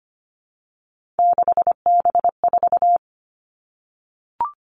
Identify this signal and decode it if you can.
Morse code, heard as a keyed tone, about 25 wpm, 700 Hz. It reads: 6B4
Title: All Cut Numbers - 3-Digit 25wpm